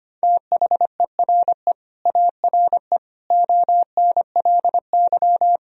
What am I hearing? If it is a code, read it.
Morse code, heard as a keyed tone, about 25 wpm, 700 Hz. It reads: THERE ARE ONLY
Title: Sentences from Top 1000 Words - Review Part 1 - 25wpm